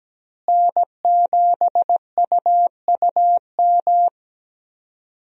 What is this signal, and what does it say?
Morse code, heard as a keyed tone, about 17 wpm, 700 Hz. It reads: N7UUM
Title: USA Call Signs - 5 Characters - Repeated 3x - 17wpm